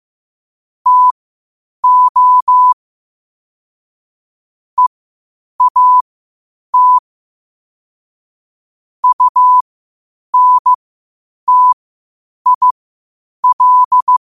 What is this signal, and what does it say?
Morse code, heard as a keyed tone, about 15 wpm, spaced as for 8 wpm, 1000 Hz. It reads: TO EAT UNTIL
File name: qotdfiles / 8wpm